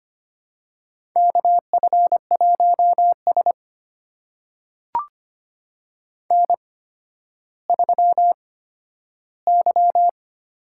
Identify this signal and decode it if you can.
Morse code, heard as a keyed tone, about 25 wpm, spaced as for 12 wpm, 700 Hz. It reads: KF1S N3Y
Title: USA Call Signs - 4 Characters - 8x Character Spacing - 25wpm